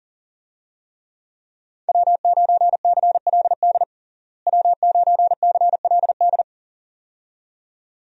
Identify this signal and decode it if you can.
Morse code, heard as a keyed tone, about 40 wpm, 700 Hz. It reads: W9CLD W9CLD
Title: USA Call Signs - 5 Characters - Repeated 2x - 40wpm